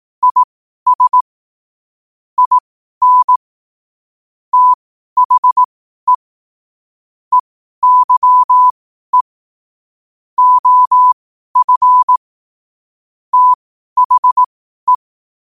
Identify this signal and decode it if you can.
Morse code, heard as a keyed tone, about 18 wpm, spaced as for 12 wpm, 1000 Hz. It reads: IS IN THE EYE OF THE
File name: qotdfiles / 12wpm